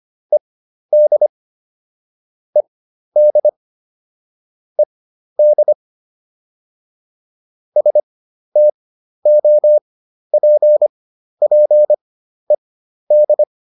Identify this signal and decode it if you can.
Morse code, heard as a keyed tone, about 25 wpm, spaced as for 12 wpm, 600 Hz. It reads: ED ED ED STOPPED